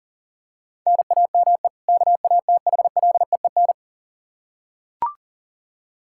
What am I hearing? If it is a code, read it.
Morse code, heard as a keyed tone, about 40 wpm, 700 Hz. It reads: NAME KATHLEEN
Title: Practice QSO - Exchange 1 - Beginning - 40wpm